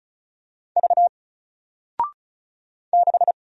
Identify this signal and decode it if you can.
Morse code, heard as a keyed tone, about 36 wpm, 700 Hz. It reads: V 6
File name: Single-Letter-Number-35wpm